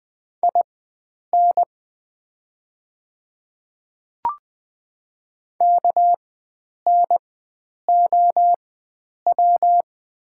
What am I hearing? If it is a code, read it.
Morse code, heard as a keyed tone, about 20 wpm, spaced as for 9 wpm, 700 Hz. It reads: IN KNOW
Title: 4x Character Spacing - Top 100 Words - 20wpm